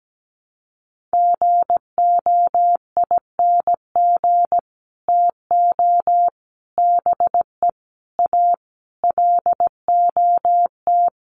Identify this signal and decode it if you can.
Morse code, heard as a keyed tone, about 17 wpm, 700 Hz. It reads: GOING TO BE A LOT
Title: Sets of 5 Words Encore 17wpm